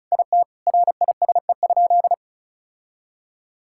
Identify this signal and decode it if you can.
Morse code, heard as a keyed tone, about 36 wpm, 700 Hz. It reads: IT RISE?
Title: Sentences from Top 1000 Words - Review Part 6 - 35wpm